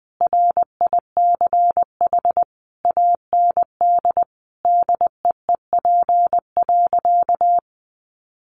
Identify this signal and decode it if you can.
Morse code, heard as a keyed tone, about 19 wpm, 700 Hz. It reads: RICH AND DEEP.
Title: Sentences from Top 1100 Words 20wpm